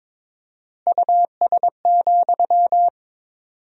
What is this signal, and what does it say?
Morse code, heard as a keyed tone, about 21 wpm, 700 Hz. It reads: US,
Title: 4x Word-Spacing - Sentences from Top 400 Words - 22wpm